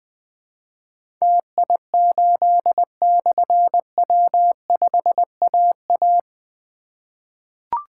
Morse code, heard as a keyed tone, about 21 wpm, 700 Hz. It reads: TI8/W5AA
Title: Difficult Call Signs 20wpm